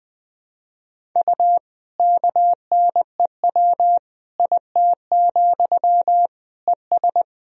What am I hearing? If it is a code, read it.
Morse code, heard as a keyed tone, about 20 wpm, 700 Hz. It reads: U KNEW IT, ES
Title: Sentences from Top 500 Words and Abbreviations 20wpm